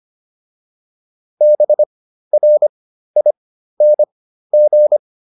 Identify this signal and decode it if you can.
Morse code, heard as a keyed tone, about 25 wpm, spaced as for 13 wpm, 600 Hz. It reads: BRING